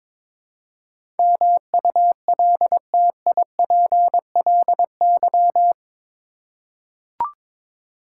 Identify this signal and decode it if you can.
Morse code, heard as a keyed tone, about 22 wpm, 700 Hz. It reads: MULTIPLY
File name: practice / 22wpm / Eight Letter Words - 22wpm